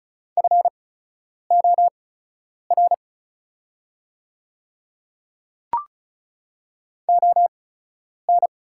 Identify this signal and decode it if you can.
Morse code, heard as a keyed tone, about 34 wpm, spaced as for 10 wpm, 700 Hz. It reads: FOR ON
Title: 8x Character Spacing - Top 100 Words - 35wpm